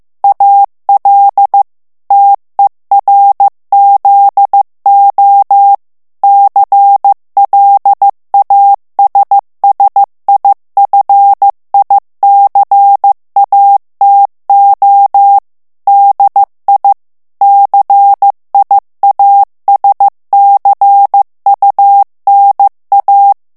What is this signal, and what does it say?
Morse code, heard as a keyed tone, about 15 wpm, 800 Hz. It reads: AL TERZO CLASSIFICATO DI CIASCUNA
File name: morse mp3 / ITALIEN texte clair 2 vitesse 14 mots minute